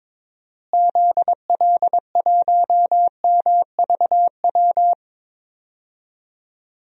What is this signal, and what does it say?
Morse code, heard as a keyed tone, about 21 wpm, 700 Hz. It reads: ZL1MVW
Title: DXCC-100 Call Signs - 6 Characters - Repeated 3x - 22wpm